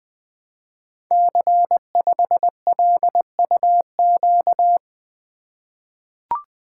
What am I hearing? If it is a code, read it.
Morse code, heard as a keyed tone, about 20 wpm, 700 Hz. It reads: C5LUQ
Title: Code Groups - 5 Letter:Numbers - 20wpm